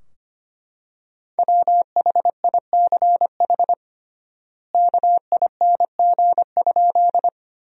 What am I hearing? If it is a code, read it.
Morse code, heard as a keyed tone, about 25 wpm, 700 Hz. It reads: WHICH KING?
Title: 3x Word-Spacing - Sentences from Top 400 Words - 25wpm